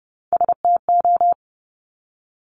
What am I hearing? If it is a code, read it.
Morse code, heard as a keyed tone, about 30 wpm, 700 Hz. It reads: STO
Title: Top Three Letter Combinations in Words 30wpm